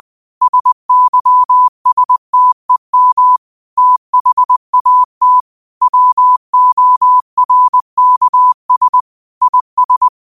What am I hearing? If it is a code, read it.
Morse code, heard as a keyed tone, about 20 wpm, 1000 Hz. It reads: SYSTEM THAT WORKS IS
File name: qotdfiles / 20wpm